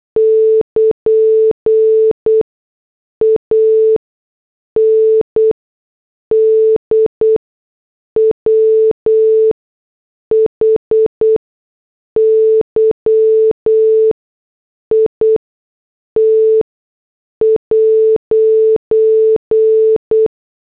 Morse code, heard as a keyed tone, about 8 wpm, spaced as for 6 wpm, 440 Hz. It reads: <KN> AND WHY IT'
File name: morse / headlines